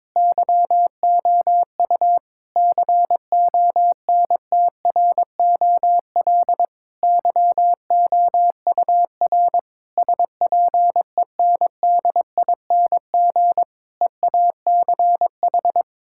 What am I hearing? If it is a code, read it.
Morse code, heard as a keyed tone, about 22 wpm, 700 Hz. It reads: YOU CONTROL YOUR SPENDING EACH
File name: Difficult Sentences with Hints - 22wpm